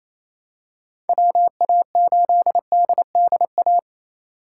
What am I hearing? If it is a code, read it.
Morse code, heard as a keyed tone, about 27 wpm, 700 Hz. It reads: WA8DDA
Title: USA Call Signs - 6 Characters - 3x Character Spacing - 28wpm